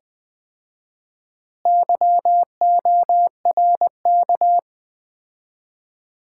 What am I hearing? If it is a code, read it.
Morse code, heard as a keyed tone, about 20 wpm, 700 Hz. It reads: YORK
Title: 4x Word-Spacing - Sets of 4 Words - 20wpm